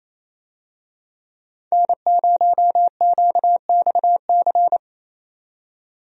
Morse code, heard as a keyed tone, about 28 wpm, 700 Hz. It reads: N0QXC
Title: USA Call Signs - 5 Characters - 28wpm